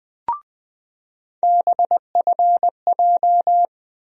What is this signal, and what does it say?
Morse code, heard as a keyed tone, about 20 wpm, 700 Hz. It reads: BFJ
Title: USA Call Sign Postfixes - 3 Letters 20wpm